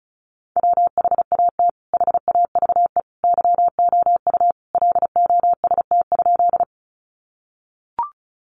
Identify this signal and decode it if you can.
Morse code, heard as a keyed tone, about 34 wpm, 700 Hz. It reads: WHAT HAVE YOU LOST?
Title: Sentences from Top 1400 Words 35wpm